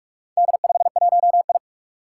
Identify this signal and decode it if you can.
Morse code, heard as a keyed tone, about 46 wpm, 700 Hz. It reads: D51I
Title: Four Challenging Letter-Number Combinations 45wpm